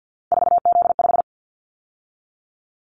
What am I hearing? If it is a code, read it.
Morse code, heard as a keyed tone, about 50 wpm, 700 Hz. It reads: 4Z5
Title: Callsign Prefixes for Top 50 DXCC - 50wpm